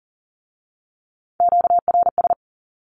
Morse code, heard as a keyed tone, about 38 wpm, 690 Hz. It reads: QRS
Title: Q-Codes with Spoken Meaning - 40wpm